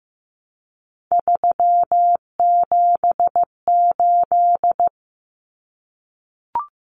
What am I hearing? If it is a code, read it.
Morse code, heard as a keyed tone, about 15 wpm, 700 Hz. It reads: 378
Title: Common Cut Numbers - 3-Digit 15wpm